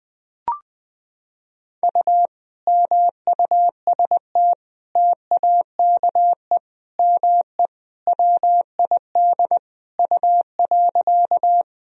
Morse code, heard as a keyed tone, about 20 wpm, 700 Hz. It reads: U MUST TAKE ME WID U.
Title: Sentences from Top 200 Words and Abbreviations 20wpm